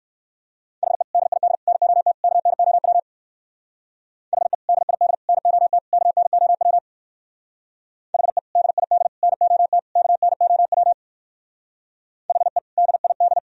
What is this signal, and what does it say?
Morse code, heard as a keyed tone, about 55 wpm, 700 Hz. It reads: HE DID NOT KNOW HE DID NOT KNOW HE DID NOT KNOW HE DID
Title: Speed-Racing - Sets of 4 Words 45wpm